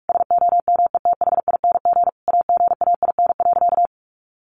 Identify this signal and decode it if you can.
Morse code, heard as a keyed tone, about 44 wpm, 700 Hz. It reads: SOMETHING AGAIN.
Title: Sentences from Top 1100 Words 45wpm